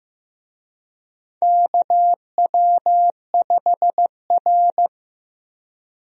Morse code, heard as a keyed tone, about 15 wpm, 700 Hz. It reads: KW5R
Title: USA Call Signs - 4 Characters - Repeated 3x - 15wpm